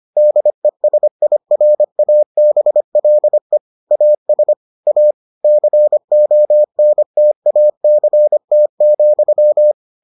Morse code, heard as a keyed tone, about 25 wpm, 600 Hz. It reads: DESIRABLE AS A CONTACT,